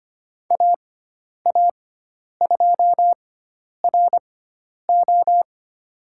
Morse code, heard as a keyed tone, about 25 wpm, spaced as for 10 wpm, 700 Hz. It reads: AA2RO
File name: Lesson 16g - Letter c - Call Signs 25wpm